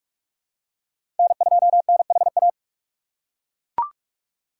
Morse code, heard as a keyed tone, about 42 wpm, 700 Hz. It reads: N2NHA
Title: USA Call Signs - 5 Characters - 4x Character Spacing - 45wpm